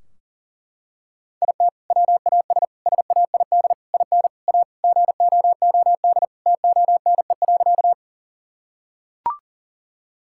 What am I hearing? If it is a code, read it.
Morse code, heard as a keyed tone, about 40 wpm, 700 Hz. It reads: IT WAS SAID IN A GOOD TONE.
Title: Sentences from Top 1600 Words 40wpm